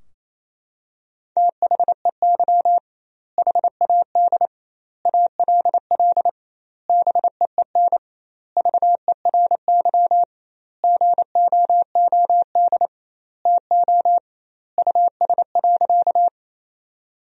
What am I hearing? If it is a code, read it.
Morse code, heard as a keyed tone, about 28 wpm, 700 Hz. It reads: THEY HAD ALL BEEN VERY GOOD TO US.
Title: 2x Word-Spacing - Sentences from Top 200 Words - 28wpm